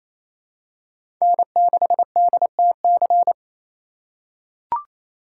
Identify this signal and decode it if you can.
Morse code, heard as a keyed tone, about 28 wpm, 700 Hz. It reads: N6DTC
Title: USA Call Signs - 5 Characters - 4x Character Spacing - 28wpm